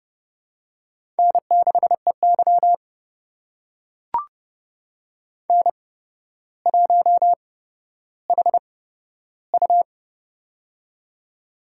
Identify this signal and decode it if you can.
Morse code, heard as a keyed tone, about 32 wpm, spaced as for 13 wpm, 700 Hz. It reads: N6EY N1HU
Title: USA Call Signs - 4 Characters - 8x Character Spacing - 30wpm